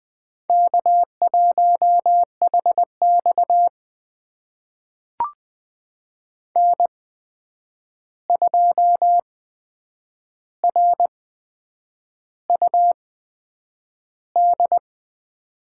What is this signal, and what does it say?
Morse code, heard as a keyed tone, about 21 wpm, spaced as for 8 wpm, 700 Hz. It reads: K1HX N2RUD
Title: USA Call Signs - 8x Character Spacing - 20wpm